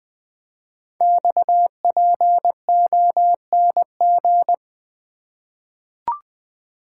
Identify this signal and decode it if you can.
Morse code, heard as a keyed tone, about 20 wpm, 700 Hz. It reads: XPONG
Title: Code Groups - 5 Letters - 20wpm